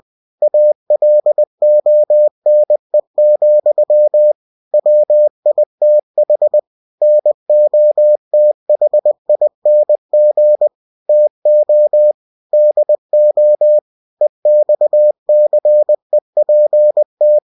Morse code, heard as a keyed tone, about 20 wpm, 600 Hz. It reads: ALONE, WITH NOTHING TO DO EXCEPT
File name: cw-books / winnie-the-pooh / winnie-20wpm-page-0090